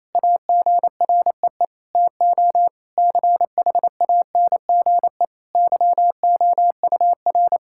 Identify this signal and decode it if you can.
Morse code, heard as a keyed tone, about 28 wpm, 700 Hz. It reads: AGREE TO CHANGE YOUR
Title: Sentences from Top 1000 Words - Review Part 6 - 28wpm